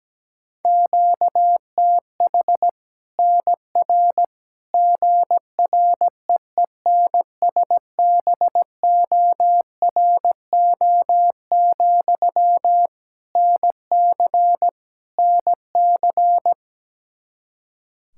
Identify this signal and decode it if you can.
Morse code, heard as a keyed tone, about 17 wpm, 700 Hz. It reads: QTH NR GREENSBORO, NC NC
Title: Practice QSO - Exchange 1 - Beginning - 17wpm